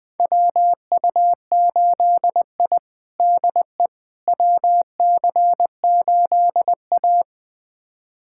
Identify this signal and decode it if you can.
Morse code, heard as a keyed tone, about 21 wpm, 700 Hz. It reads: WU8I DE WC8A
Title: USA Callsign DE Callsign - 4 Characters - 20wpm